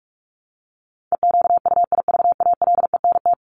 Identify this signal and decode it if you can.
Morse code, heard as a keyed tone, about 44 wpm, 700 Hz. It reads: EQUIVALENT
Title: Top 1501-1750 Words 45wpm